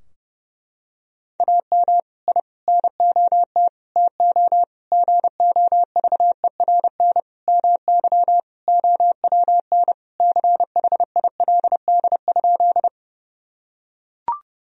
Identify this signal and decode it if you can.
Morse code, heard as a keyed tone, about 30 wpm, 700 Hz. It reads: AM I NOT TO GOVERN MY OWN CHILD?
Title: Sentences from Top 1000 Words - Review Part 2 - 30wpm